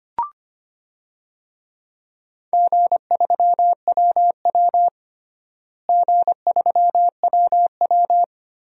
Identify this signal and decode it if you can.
Morse code, heard as a keyed tone, about 25 wpm, 700 Hz. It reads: G3WW G3WW
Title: DXCC-100 Call Signs - 4 Characters - Repeated 3x - 25wpm